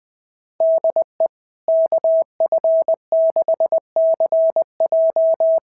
Morse code, heard as a keyed tone, about 18 wpm, 650 Hz. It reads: DE KF6CJ
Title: SOTA QSO Practice 20wpm